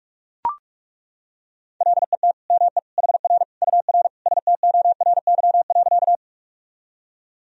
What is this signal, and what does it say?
Morse code, heard as a keyed tone, about 42 wpm, 700 Hz. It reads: LET ME HR UR STORY.
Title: Sentences from Top 300 Words and Abbreviations 45wpm